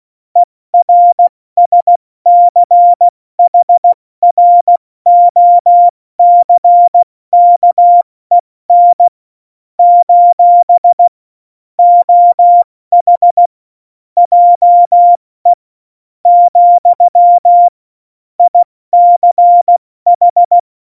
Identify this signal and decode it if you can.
Morse code, heard as a keyed tone, about 16 wpm, spaced as for 13 wpm, 700 Hz. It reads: ERSCHROCKEN: OH JE, ICH
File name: Morsekurs / Witze / Witz 16. Oktober 2018, 70 F80